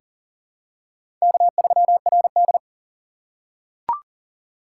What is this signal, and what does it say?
Morse code, heard as a keyed tone, about 40 wpm, 700 Hz. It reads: K3RD
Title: USA Call Signs - 4 Characters - 2x Character Spacing - 40wpm